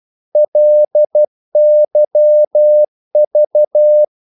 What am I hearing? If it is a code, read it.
Morse code, heard as a keyed tone, about 12 wpm, 600 Hz. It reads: LYV
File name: Korte ord DK 12wpm